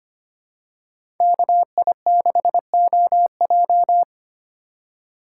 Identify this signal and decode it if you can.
Morse code, heard as a keyed tone, about 25 wpm, 700 Hz. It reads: KI6OJ
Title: USA Call Signs - 5 Characters - 2x Character Spacing - 25wpm